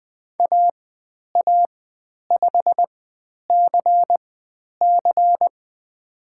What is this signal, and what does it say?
Morse code, heard as a keyed tone, about 20 wpm, spaced as for 10 wpm, 700 Hz. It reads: AA5CC
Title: 095 Lesson 16g - Letter c - Call Signs 20wpm